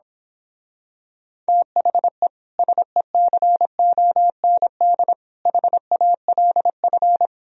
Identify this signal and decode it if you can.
Morse code, heard as a keyed tone, about 26 wpm, 700 Hz. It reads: THE SECOND HALF
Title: Speed-Racing - Sets of 5 Words 22wpm